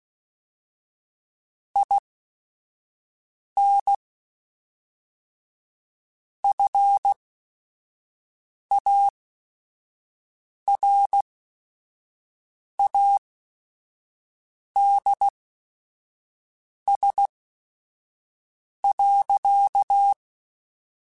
Morse code, heard as a keyed tone, about 16 wpm, spaced as for 5 wpm, 800 Hz. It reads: IN FARADS.
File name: Technical Subjects / Technical Subjects-mp3s / XC @5